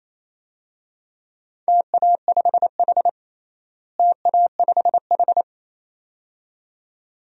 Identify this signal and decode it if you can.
Morse code, heard as a keyed tone, about 27 wpm, 700 Hz. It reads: TA5H TA5H
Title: DXCC-100 Call Signs - 4 Characters - Repeated 2x - 28wpm